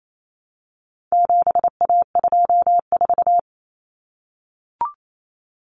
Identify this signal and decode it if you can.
Morse code, heard as a keyed tone, about 28 wpm, 700 Hz. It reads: 7A24
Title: Common Cut Numbers - 4-Digit 28wpm